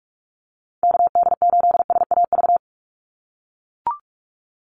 Morse code, heard as a keyed tone, about 44 wpm, 700 Hz. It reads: KD8SAV
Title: Call Signs - 45wpm